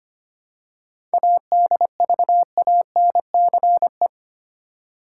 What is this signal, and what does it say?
Morse code, heard as a keyed tone, about 25 wpm, 700 Hz. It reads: ADVANCE
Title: Seven Letter Words - 25wpm